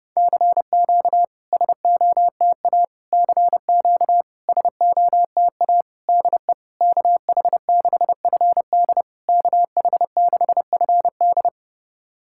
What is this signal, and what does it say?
Morse code, heard as a keyed tone, about 29 wpm, 690 Hz. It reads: CQ SOTA CQ SOTA DE KH6FD KH6FD
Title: SOTA QSO Practice 30wpm